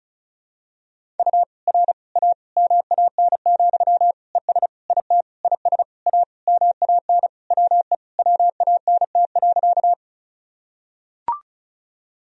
Morse code, heard as a keyed tone, about 34 wpm, 700 Hz. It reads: U R A MAN, ES IT IS A MAN WE WANT.
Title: Sentences from Top 200 Words and Abbreviations 35wpm